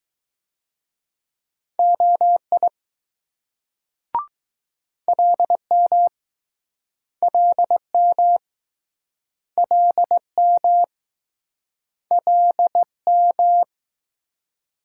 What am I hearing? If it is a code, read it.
Morse code, heard as a keyed tone, about 20 wpm, 700 Hz. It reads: OI LM LM LM LM